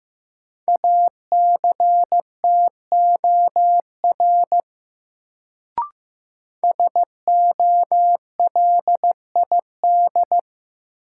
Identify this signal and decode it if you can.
Morse code, heard as a keyed tone, about 15 wpm, 700 Hz. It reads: ACTOR SOLID
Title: Sentences from Top 1500 Words - New Word Drill - 15wpm